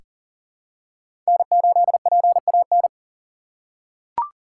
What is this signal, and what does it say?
Morse code, heard as a keyed tone, about 40 wpm, 700 Hz. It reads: N8PAN
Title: USA Call Signs - 3x Character Spacing - 40wpm